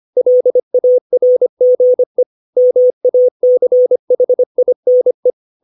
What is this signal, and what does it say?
Morse code, heard as a keyed tone, about 25 wpm, 500 Hz. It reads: LARGE MACHINE
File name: 632 words / 632 Words 25WPM